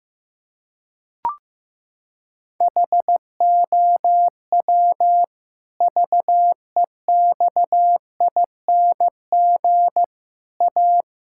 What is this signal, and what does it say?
Morse code, heard as a keyed tone, about 15 wpm, 700 Hz. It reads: HOW VEXING A
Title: Pangram Challenge v2 - 15wpm